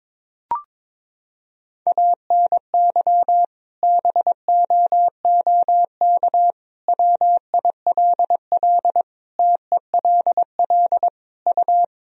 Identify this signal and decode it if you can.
Morse code, heard as a keyed tone, about 22 wpm, 700 Hz. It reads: ANY BOOK WILL TELL U
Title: Sentences from Top 300 Words and Abbreviations 22wpm